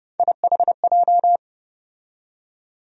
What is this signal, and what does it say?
Morse code, heard as a keyed tone, about 29 wpm, 700 Hz. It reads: IHJ
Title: USA Call Sign Postfixes - 3 Letters 30wpm